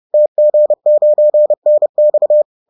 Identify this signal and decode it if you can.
Morse code, heard as a keyed tone, about 30 wpm, 600 Hz. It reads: TG9NX